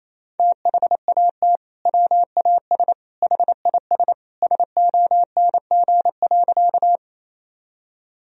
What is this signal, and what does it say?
Morse code, heard as a keyed tone, about 28 wpm, 700 Hz. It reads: THAT WAS HIS SONG.
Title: Sentences from Top 1000 Words - Review Part 2 - 28wpm